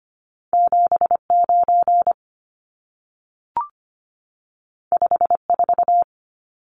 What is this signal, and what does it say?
Morse code, heard as a keyed tone, about 25 wpm, 700 Hz. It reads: 79 54
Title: Numbers - 2-Digit 25wpm